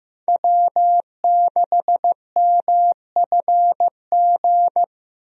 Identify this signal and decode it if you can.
Morse code, heard as a keyed tone, about 15 wpm, 700 Hz. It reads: W6MFG
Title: USA Call Signs - 5 Characters - Repeated 3x - 15wpm